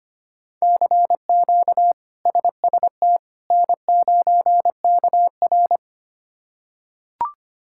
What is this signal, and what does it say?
Morse code, heard as a keyed tone, about 25 wpm, 700 Hz. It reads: CQ SST N9KR
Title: K1USN SST QSO Practice - 25wpm